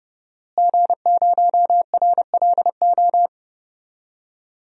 Morse code, heard as a keyed tone, about 30 wpm, 700 Hz. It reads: G0RLO
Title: DXCC-100 Call Signs - 5 Characters - Repeated 3x - 30wpm